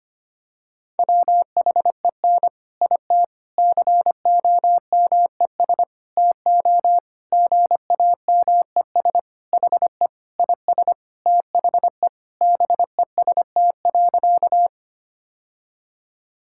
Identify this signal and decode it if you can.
Morse code, heard as a keyed tone, about 25 wpm, 700 Hz. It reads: WHEN IT COMES TO GAMES HE IS THE BEST.
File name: practice / 25wpm / Sentences with 3-Word Phrases - 25wpm